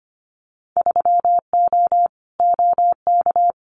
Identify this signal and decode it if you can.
Morse code, heard as a keyed tone, about 24 wpm, 700 Hz. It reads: 3O OK
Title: Winter Field Day Exchanges Encore 25wpm